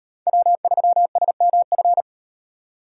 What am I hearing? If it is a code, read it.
Morse code, heard as a keyed tone, about 38 wpm, 700 Hz. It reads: W3SMF